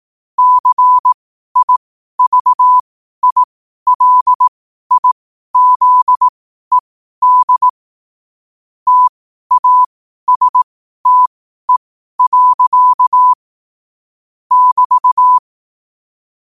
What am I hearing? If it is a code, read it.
Morse code, heard as a keyed tone, about 18 wpm, spaced as for 12 wpm, 1000 Hz. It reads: CIVILIZED TASTE. <BT>
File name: qotdfiles / 12wpm